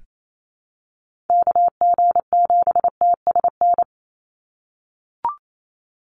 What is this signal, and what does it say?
Morse code, heard as a keyed tone, about 28 wpm, 700 Hz. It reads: KG7TSN
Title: USA Call Signs 28wpm